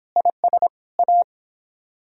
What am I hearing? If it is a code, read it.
Morse code, heard as a keyed tone, about 29 wpm, 700 Hz. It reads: IS A